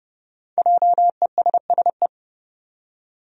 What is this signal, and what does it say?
Morse code, heard as a keyed tone, about 30 wpm, 700 Hz. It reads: JESSE
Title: USA Common Names - 30wpm